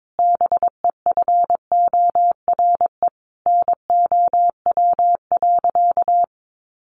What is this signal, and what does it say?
Morse code, heard as a keyed tone, about 22 wpm, 700 Hz. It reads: BEFORE NOW.
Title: Sentences from Top 1300 Words 22wpm